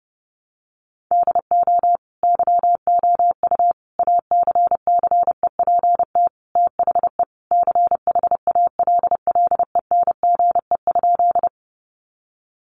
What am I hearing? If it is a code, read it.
Morse code, heard as a keyed tone, about 30 wpm, 700 Hz. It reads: DO YOU ACCEPT THE CHALLENGE?
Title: Sentences from Top 1300 Words 30wpm